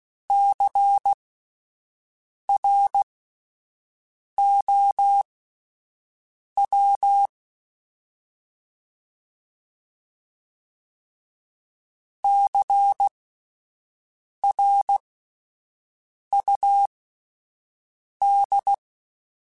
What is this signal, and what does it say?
Morse code, heard as a keyed tone, about 16 wpm, spaced as for 5 wpm, 800 Hz. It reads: CROW CRUD
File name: Spaced Alphas / Spaced Alphas mp3s / Cs sp @10